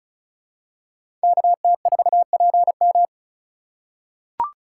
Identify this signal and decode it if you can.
Morse code, heard as a keyed tone, about 34 wpm, 700 Hz. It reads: KT4PM
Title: USA Call Signs - 5 Characters - 2x Character Spacing - 35wpm